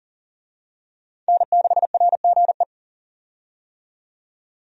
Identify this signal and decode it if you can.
Morse code, heard as a keyed tone, about 37 wpm, 700 Hz. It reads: N6RGE
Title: USA Call Signs - 5 Characters - 40wpm